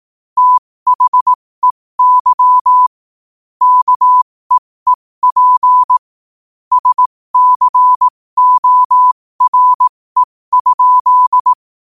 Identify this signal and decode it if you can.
Morse code, heard as a keyed tone, about 18 wpm, spaced as for 15 wpm, 1000 Hz. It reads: THEY KEEP SCORE?